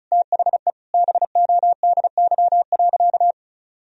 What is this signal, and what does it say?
Morse code, heard as a keyed tone, about 36 wpm, 700 Hz. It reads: THE BODY.
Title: Sentences from Top 1000 Words - Review Part 4 - 35wpm